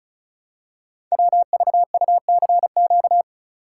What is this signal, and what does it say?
Morse code, heard as a keyed tone, about 36 wpm, 700 Hz. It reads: WVUCQ
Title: Code Groups - 5 Letters - 35wpm